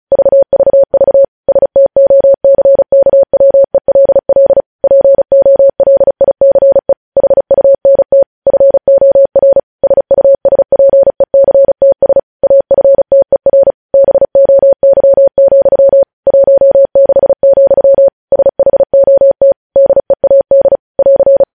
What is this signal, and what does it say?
Morse code, heard as a keyed tone, about 36 wpm, 550 Hz. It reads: VVV STOCKWELL POLICE HUNT FOR SUSPECTS AFTER BOY, 16, SHOT DEAD <AR>
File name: podcast / News headlines-1x-35x35